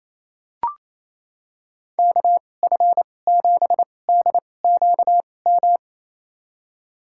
Morse code, heard as a keyed tone, about 28 wpm, spaced as for 20 wpm, 700 Hz. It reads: KF7DQM
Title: USA Call Signs - 6 Characters - 2x Character Spacing - 28wpm